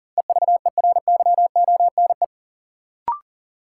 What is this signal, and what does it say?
Morse code, heard as a keyed tone, about 37 wpm, 700 Hz. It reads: EVERYONE
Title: Sentences from Top 800 Words - New Word Drill - 40wpm